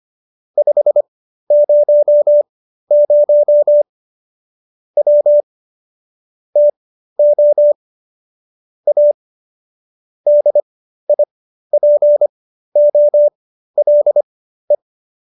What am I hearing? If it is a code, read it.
Morse code, heard as a keyed tone, about 25 wpm, spaced as for 13 wpm, 590 Hz. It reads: 500 W TO A DIPOLE